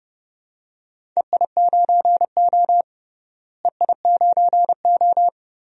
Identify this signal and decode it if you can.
Morse code, heard as a keyed tone, about 30 wpm, 700 Hz. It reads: EI9O EI9O
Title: DXCC-100 Call Signs - 4 Characters - Repeated 2x - 30wpm